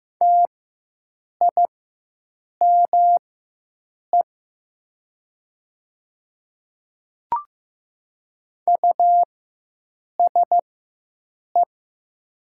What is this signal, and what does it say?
Morse code, heard as a keyed tone, about 15 wpm, spaced as for 7 wpm, 700 Hz. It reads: TIME USE
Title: 4x Character Spacing - Top 100 Words - 15wpm